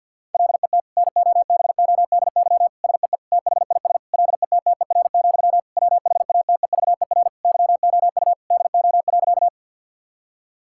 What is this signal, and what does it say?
Morse code, heard as a keyed tone, about 50 wpm, 700 Hz. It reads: LET NOBODY SEE THIS LETTER, WHATEVER YOU DO.
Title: Sentences from Top 1600 Words 50wpm